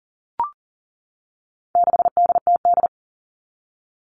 Morse code, heard as a keyed tone, about 40 wpm, 700 Hz. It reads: 6DTD
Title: All Cut Numbers - 4-Digit 40wpm